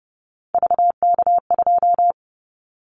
Morse code, heard as a keyed tone, about 30 wpm, 700 Hz. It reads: VK2
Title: Callsign Prefixes for Top 100 DXCC - 30wpm